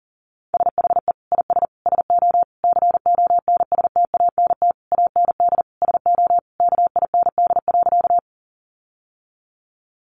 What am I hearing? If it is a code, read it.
Morse code, heard as a keyed tone, about 40 wpm, 700 Hz. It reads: SHE IS SO CONSTANT AND SO KIND.
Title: Sentences from Top 1500 Words 40wpm